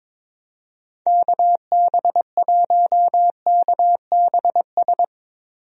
Morse code, heard as a keyed tone, about 22 wpm, 700 Hz. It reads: KB1KBS
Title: USA Call Signs - 6 Characters - Repeated 3x - 22wpm